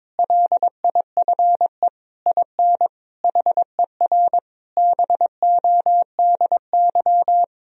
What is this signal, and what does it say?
Morse code, heard as a keyed tone, about 22 wpm, 700 Hz. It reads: LIFE IN HER BODY
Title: Sentences from Top 1000 Words - Review Part 2 - 22wpm